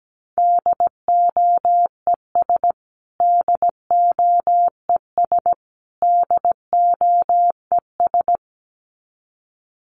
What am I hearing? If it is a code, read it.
Morse code, heard as a keyed tone, about 17 wpm, 700 Hz. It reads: DOES DOES DOES
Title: Top 200 Words Repeated 17wpm